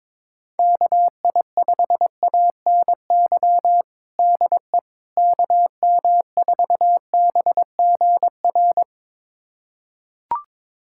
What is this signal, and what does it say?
Morse code, heard as a keyed tone, about 21 wpm, 700 Hz. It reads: KI5ANY DE KM4BGR
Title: USA Callsign DE Callsign 22wpm